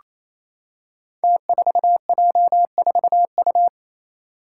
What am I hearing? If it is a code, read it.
Morse code, heard as a keyed tone, about 28 wpm, 700 Hz. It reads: T4J4U
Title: Five Challenging Letter-Number Combinations 28wpm